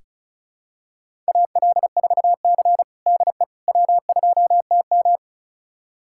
Morse code, heard as a keyed tone, about 36 wpm, 700 Hz. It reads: AL4C DE W2TM
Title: USA Callsign DE Callsign - 4 Characters - 35wpm